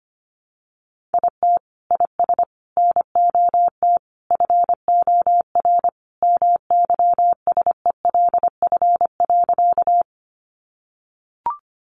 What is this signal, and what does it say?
Morse code, heard as a keyed tone, about 25 wpm, 700 Hz. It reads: IT IS NOT FOR MYSELF.
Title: Sentences from Top 1300 Words 25wpm